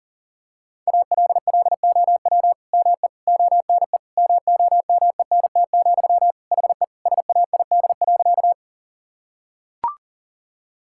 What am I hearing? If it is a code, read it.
Morse code, heard as a keyed tone, about 37 wpm, 700 Hz. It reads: ALLOW ME ONE MOMENT, HE SAID.
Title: Sentences from Top 1000 Words - Review Part 5 - 40wpm